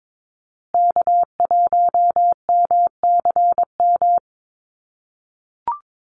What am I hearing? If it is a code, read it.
Morse code, heard as a keyed tone, about 22 wpm, 700 Hz. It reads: K1MCM
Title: USA Call Signs Repeated 3x 22wpm